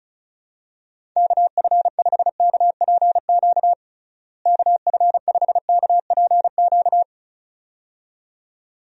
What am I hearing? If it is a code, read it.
Morse code, heard as a keyed tone, about 34 wpm, 700 Hz. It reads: KF5KPQ KF5KPQ
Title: USA Call Signs - 6 Characters - Repeated 2x - 35wpm